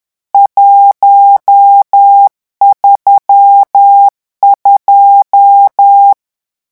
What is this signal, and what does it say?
Morse code, heard as a keyed tone, about 10 wpm, 800 Hz. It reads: 132
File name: morse mp3 / lecon10-vitesse 10